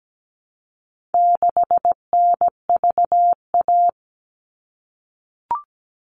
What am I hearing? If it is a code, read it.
Morse code, heard as a keyed tone, about 17 wpm, 700 Hz. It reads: 6NVA
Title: All Cut Numbers - 4-Digit 17wpm